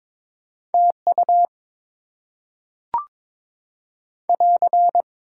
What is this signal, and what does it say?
Morse code, heard as a keyed tone, about 22 wpm, 700 Hz. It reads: TU <AR>
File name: Top 90 QSO Elements - 8x Character Spacing - 22wpm